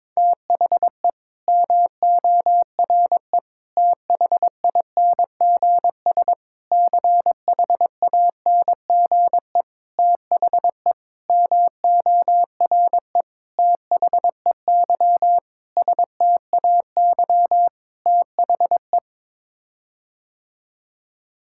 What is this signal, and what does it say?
Morse code, heard as a keyed tone, about 22 wpm, 700 Hz. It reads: THE MORE THINGS CHANGE THE MORE THEY STAY THE
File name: Sets of 9 Words - 22wpm